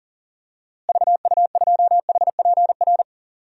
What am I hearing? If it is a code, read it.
Morse code, heard as a keyed tone, about 39 wpm, 700 Hz. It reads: VU2HPR
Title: DXCC-100 Call Signs - 6 Characters - Repeated 2x - 40wpm